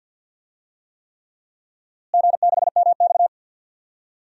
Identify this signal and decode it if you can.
Morse code, heard as a keyed tone, about 53 wpm, 700 Hz. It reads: G6MX
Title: DXCC-100 Call Signs - 4 Characters - Repeated 3x - 50wpm